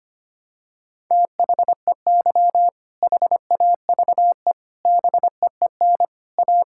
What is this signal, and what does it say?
Morse code, heard as a keyed tone, about 25 wpm, 700 Hz. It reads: THEY HAVE BEEN A
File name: practice / 25wpm / Sentences with 3-Word Phrases - 25wpm